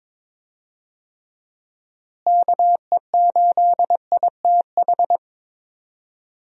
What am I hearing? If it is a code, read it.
Morse code, heard as a keyed tone, about 23 wpm, 700 Hz. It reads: KE8ITH
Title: USA Call Signs - 6 Characters - Repeated 2x - 22wpm